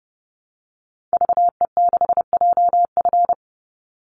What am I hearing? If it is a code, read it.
Morse code, heard as a keyed tone, about 30 wpm, 700 Hz. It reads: VE6JF
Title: Call Signs - 30wpm